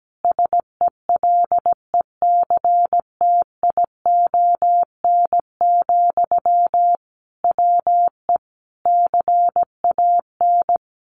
Morse code, heard as a keyed tone, about 17 wpm, 700 Hz. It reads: SELECTION, WE CAN
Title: Sentences from Top 1400 Words 17wpm